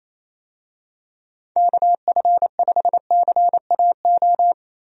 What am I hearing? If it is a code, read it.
Morse code, heard as a keyed tone, about 28 wpm, 700 Hz. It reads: KF5CAO
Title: USA Call Signs - 6 Characters - Repeated 3x - 28wpm